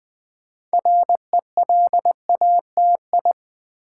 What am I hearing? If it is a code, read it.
Morse code, heard as a keyed tone, about 20 wpm, 700 Hz. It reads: RELATI
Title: Build A Word - 3 Letters At A Time - 8-Letter Words - 20wpm